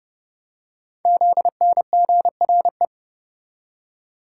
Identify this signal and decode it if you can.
Morse code, heard as a keyed tone, about 30 wpm, 700 Hz. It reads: ZNGRE
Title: Code Groups - 5 Letters - 30wpm